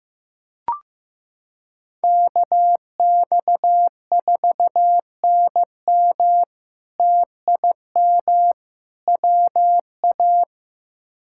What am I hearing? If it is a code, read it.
Morse code, heard as a keyed tone, about 15 wpm, 700 Hz. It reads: KX4NM TIM WA
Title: K1USN SST QSO Practice - 15wpm